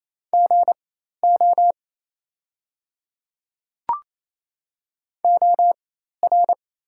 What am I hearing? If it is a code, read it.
Morse code, heard as a keyed tone, about 28 wpm, spaced as for 13 wpm, 690 Hz. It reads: GO OR
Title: 4x Character Spacing - Top 200 Words - 28wpm